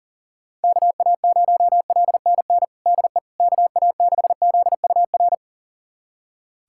Morse code, heard as a keyed tone, about 40 wpm, 700 Hz. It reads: KA0LNN DE KA6ZUR
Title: USA Callsign DE Callsign - 6 Characters - 40wpm